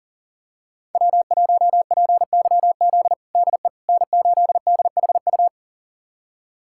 Morse code, heard as a keyed tone, about 40 wpm, 700 Hz. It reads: W1PYZ DE N8DHU
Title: USA Callsign DE Callsign - 5 Characters - 40wpm